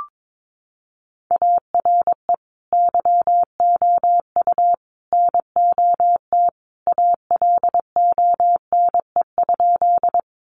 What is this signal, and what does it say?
Morse code, heard as a keyed tone, about 22 wpm, 700 Hz. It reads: ARE YOU NOT ALONE?
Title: Sentences from Top 1000 Words 22wpm